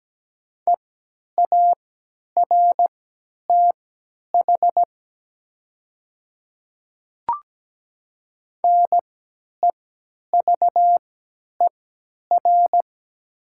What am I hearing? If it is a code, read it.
Morse code, heard as a keyed tone, about 17 wpm, spaced as for 10 wpm, 700 Hz. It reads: EARTH NEVER
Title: 3x Character Spacing - Top 300 Words - 17wpm